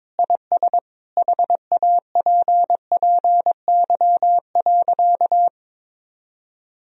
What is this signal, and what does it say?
Morse code, heard as a keyed tone, about 22 wpm, 700 Hz. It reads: IS HAPPY.
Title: Sentences from Top 1000 Words - Review Part 3 - 22wpm